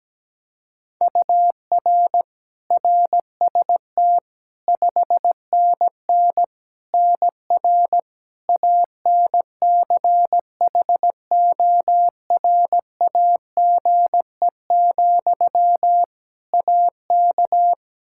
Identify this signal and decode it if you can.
Morse code, heard as a keyed tone, about 17 wpm, 700 Hz. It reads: UR RST 5NN NR ANCHORAGE, AK
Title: Practice QSO - Exchange 1 - Beginning - 17wpm